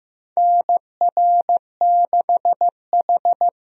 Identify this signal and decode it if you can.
Morse code, heard as a keyed tone, about 15 wpm, 700 Hz. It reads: NR6H
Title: USA Call Signs - 4 Characters - 15wpm